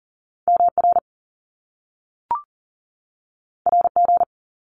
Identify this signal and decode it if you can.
Morse code, heard as a keyed tone, about 39 wpm, 700 Hz. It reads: MR RG
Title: Top Two Letter Combinations in Words 40wpm